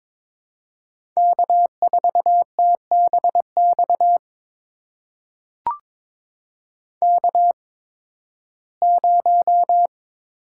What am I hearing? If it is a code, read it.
Morse code, heard as a keyed tone, about 22 wpm, spaced as for 14 wpm, 700 Hz. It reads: K4TBX K0
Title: USA Call Signs - 5 Characters - 8x Character Spacing - 22wpm